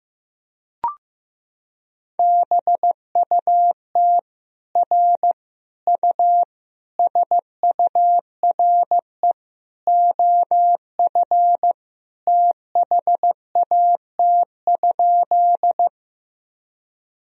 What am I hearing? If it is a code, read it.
Morse code, heard as a keyed tone, about 15 wpm, 700 Hz. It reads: BUT R U SURE OF THAT?
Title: Sentences from Top 300 Words and Abbreviations 15wpm